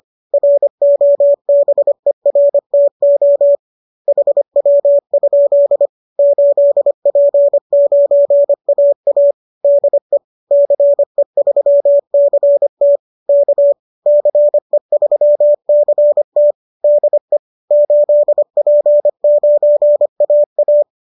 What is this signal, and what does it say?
Morse code, heard as a keyed tone, about 25 wpm, 570 Hz. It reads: ROBERTO HW? 8P9AA DE CE3CT K CE3CT DE 8P9AA